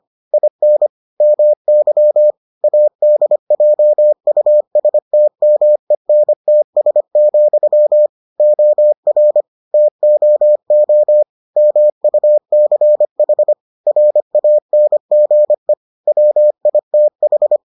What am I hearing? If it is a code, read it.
Morse code, heard as a keyed tone, about 25 wpm, 600 Hz. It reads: IN MY ADJUSTMENTS, OR TOO MUCH RANGE WITH